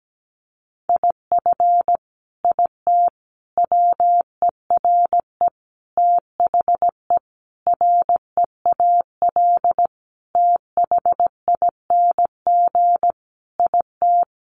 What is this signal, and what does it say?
Morse code, heard as a keyed tone, about 17 wpm, 700 Hz. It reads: IF IT WERE THE REAL THING IT
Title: Sentences from Top 1300 Words 17wpm